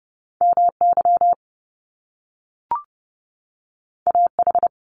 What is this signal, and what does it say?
Morse code, heard as a keyed tone, about 30 wpm, 700 Hz. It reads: MY AH
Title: Top Two Letter Combinations in Words 30wpm